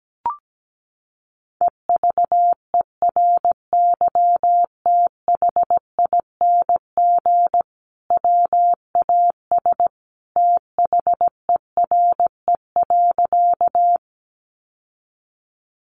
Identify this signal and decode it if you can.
Morse code, heard as a keyed tone, about 17 wpm, 700 Hz. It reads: EVERYTHING WAS THERE.
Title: Sentences from Top 700 Words 17wpm